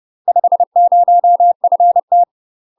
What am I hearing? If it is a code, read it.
Morse code, heard as a keyed tone, about 30 wpm, 700 Hz. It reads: 50FT
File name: UK MASTER QSOs TRAINING / 30WPM / UK MASTER QSOs 0001